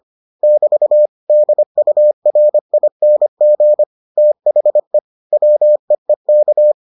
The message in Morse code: <BT> DURING THE WEEK

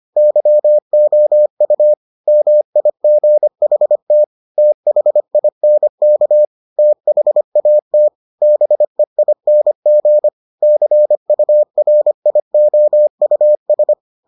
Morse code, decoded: YOU MIGHT THINK THAT BEING CURIOUS

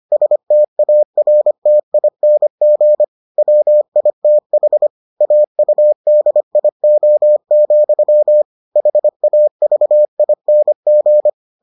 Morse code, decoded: STARTING WITH AUDIO, HAVING

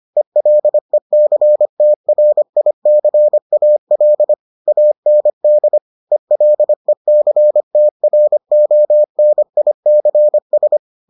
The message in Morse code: ELECTRICAL AND ELECTRONICS

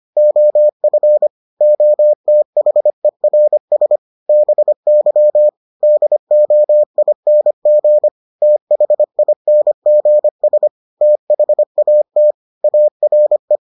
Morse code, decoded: OF OTHERS BY DOING THINGS THAT ARE

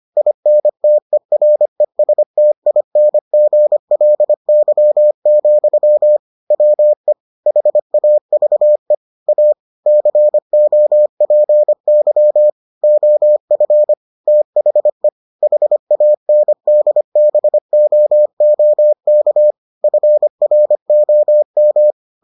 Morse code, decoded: INTERESTINGLY, WE HAVE A COPY OF THE HANDBOOK FROM

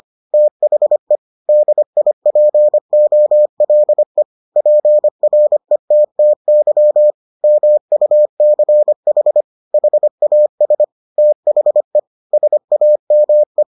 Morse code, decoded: THE DIPOLE PRETTY MUCH HAS THE SAME

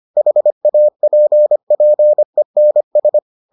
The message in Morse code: HAPPENS

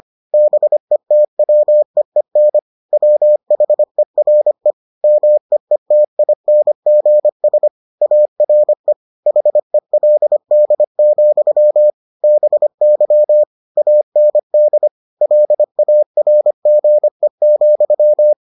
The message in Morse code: BETWEEN WHERE MEETINGS ARE HELD, BY AND LARGE,

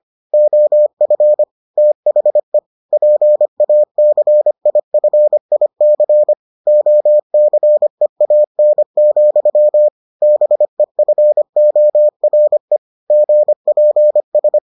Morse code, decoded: OF THE PACIFIC OCEAN, BEFORE GPS

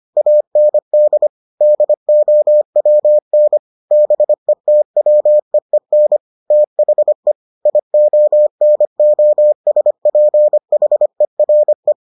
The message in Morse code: AND DOWN BETWEEN THE IONOSPHERE

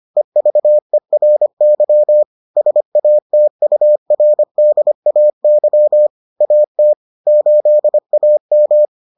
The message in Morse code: EVERY SATURDAY AT 8AM

